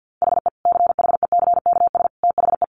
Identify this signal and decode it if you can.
Morse code, heard as a keyed tone, about 48 wpm, 700 Hz. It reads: HE CHECKS THE
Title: Easy Phrases 50wpm